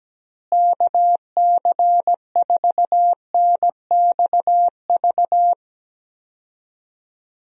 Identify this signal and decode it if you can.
Morse code, heard as a keyed tone, about 17 wpm, 700 Hz. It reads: KC4NXV
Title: USA Call Signs - 6 Characters - Repeated 3x - 17wpm